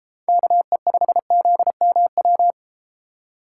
Morse code, heard as a keyed tone, about 32 wpm, 700 Hz. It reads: KE5ZMW